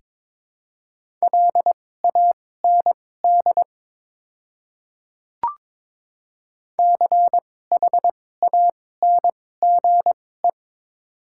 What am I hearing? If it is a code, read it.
Morse code, heard as a keyed tone, about 23 wpm, spaced as for 16 wpm, 700 Hz. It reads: LAND CHANGE
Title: 2x Character Spacing - Top 200 Words - 22wpm